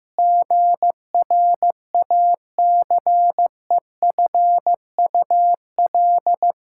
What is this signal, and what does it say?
Morse code, heard as a keyed tone, about 15 wpm, 700 Hz. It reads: GRACEFUL
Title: Pangram Challenge v2 - 15wpm